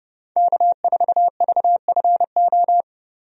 Morse code, heard as a keyed tone, about 30 wpm, 700 Hz. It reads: K4VFO
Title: USA Call Signs - 5 Characters - 30wpm